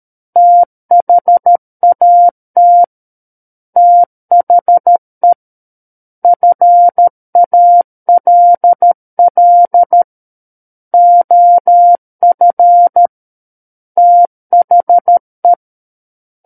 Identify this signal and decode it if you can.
Morse code, heard as a keyed tone, about 13 wpm, 700 Hz. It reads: THAT THE FALL OF THE
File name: Book One / 04-Chapter 4 - 13 WPM